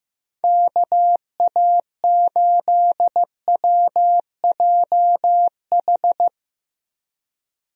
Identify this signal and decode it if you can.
Morse code, heard as a keyed tone, about 15 wpm, 700 Hz. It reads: KA8WJH
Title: USA Call Signs - 6 Characters - 15wpm